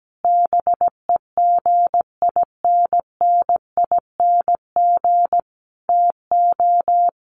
Morse code, heard as a keyed tone, about 17 wpm, 700 Hz. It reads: BEGINNING TO
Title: Sentences from Top 1300 Words 17wpm